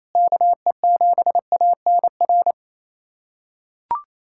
Morse code, heard as a keyed tone, about 28 wpm, 700 Hz. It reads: KE7ANR
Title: USA Callsign DE Callsign - 6 Characters - 28wpm